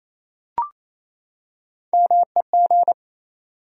Morse code, heard as a keyed tone, about 27 wpm, 700 Hz. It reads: MEG
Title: USA Common Names - Max 5 Letters - 28wpm